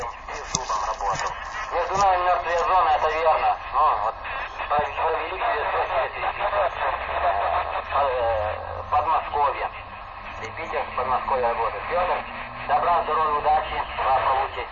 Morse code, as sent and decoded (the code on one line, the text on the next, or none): none